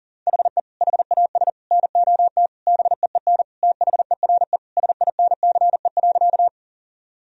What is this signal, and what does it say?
Morse code, heard as a keyed tone, about 37 wpm, 700 Hz. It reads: HE HAS NOT BEEN THERE SINCE.
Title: Sentences from Top 1000 Words - Review Part 1 - 40wpm